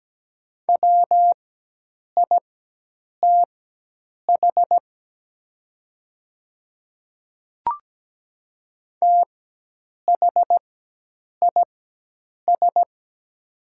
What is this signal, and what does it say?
Morse code, heard as a keyed tone, about 17 wpm, spaced as for 8 wpm, 700 Hz. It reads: WITH THIS
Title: 4x Character Spacing - Top 100 Words - 17wpm